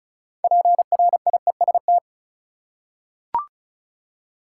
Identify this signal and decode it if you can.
Morse code, heard as a keyed tone, about 36 wpm, 700 Hz. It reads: PRIEST
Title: Six Letter Words - 35wpm